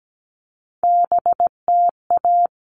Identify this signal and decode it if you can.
Morse code, heard as a keyed tone, about 17 wpm, 700 Hz. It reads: BTA
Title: All Cut Numbers - 3-Digit 17wpm